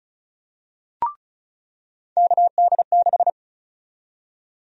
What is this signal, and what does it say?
Morse code, heard as a keyed tone, about 36 wpm, 700 Hz. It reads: KD6
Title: USA Call Sign Prefixes - 35wpm